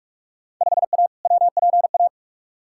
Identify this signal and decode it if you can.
Morse code, heard as a keyed tone, about 41 wpm, 700 Hz. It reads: 5A WPA
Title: Field Day Exchanges Encore - 45wpm